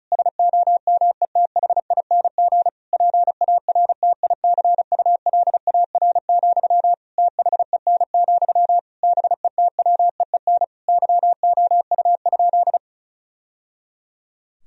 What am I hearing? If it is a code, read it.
Morse code, heard as a keyed tone, about 36 wpm, 700 Hz. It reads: SOMETHING PARTICULAR, THEN, BETWEEN YOU?
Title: Sentences from Top 1000 Words - Review Part 4 - 35wpm